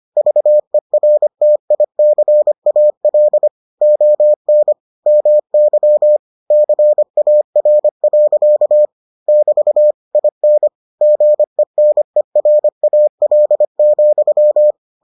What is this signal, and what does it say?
Morse code, heard as a keyed tone, about 25 wpm, 600 Hz. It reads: VERTICAL ON MY CAR. <BT> IN GENERAL,